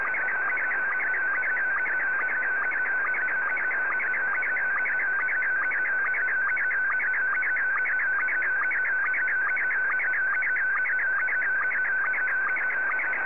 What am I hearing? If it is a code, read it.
SSTV (slow-scan television), Scottie 1.